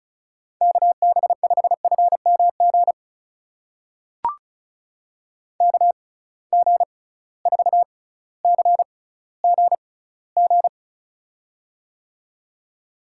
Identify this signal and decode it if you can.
Morse code, heard as a keyed tone, about 36 wpm, spaced as for 18 wpm, 700 Hz. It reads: KB5FMG KG4CGG
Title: USA Call Signs - 6x Character Spacing - 35wpm